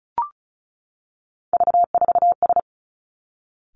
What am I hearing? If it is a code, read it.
Morse code, heard as a keyed tone, about 36 wpm, 700 Hz. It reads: V4S
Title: Three Challenging Letter-Number Combinations 35wpm